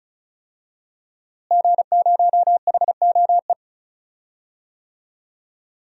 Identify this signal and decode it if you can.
Morse code, heard as a keyed tone, about 35 wpm, 700 Hz. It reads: G0HOE
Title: DXCC-100 Call Signs - 5 Characters - Repeated 3x - 35wpm